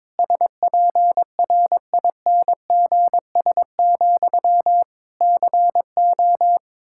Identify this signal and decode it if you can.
Morse code, heard as a keyed tone, about 22 wpm, 700 Hz. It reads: SPRINGS, CO
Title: Practice QSO - Exchange 1 - Beginning - 22wpm